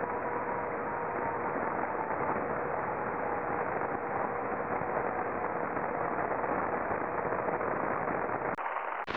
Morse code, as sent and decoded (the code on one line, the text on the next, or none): none